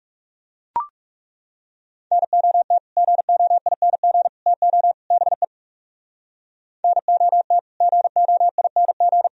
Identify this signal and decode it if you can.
Morse code, heard as a keyed tone, about 36 wpm, 700 Hz. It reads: NOT GOING TO BE NOT GOING
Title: Speed-Racing - Sets of 4 Words 30wpm